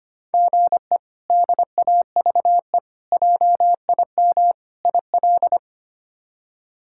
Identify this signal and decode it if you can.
Morse code, heard as a keyed tone, about 25 wpm, 700 Hz. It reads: GE DAVE JIM IL
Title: K1USN SST QSO Practice - 25wpm